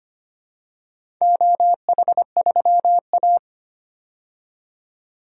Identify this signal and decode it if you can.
Morse code, heard as a keyed tone, about 25 wpm, 700 Hz. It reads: OH3A
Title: DXCC-100 Call Signs - 4 Characters - 25wpm